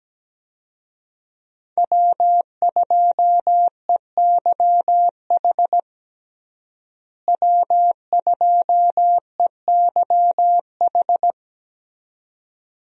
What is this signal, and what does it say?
Morse code, heard as a keyed tone, about 17 wpm, 700 Hz. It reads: W2EYH W2EYH
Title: USA Call Signs - 5 Characters - Repeated 2x - 17wpm